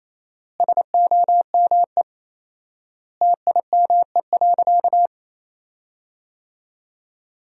Morse code, heard as a keyed tone, about 28 wpm, 700 Hz. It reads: SOME TIME.
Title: 4x Word-Spacing - Sentences from Top 300 Words - 28wpm